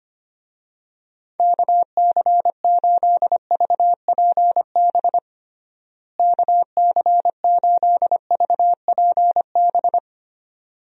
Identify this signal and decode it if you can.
Morse code, heard as a keyed tone, about 25 wpm, 700 Hz. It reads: KC8VPB KC8VPB
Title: USA Call Signs - 6 Characters - Repeated 2x - 25wpm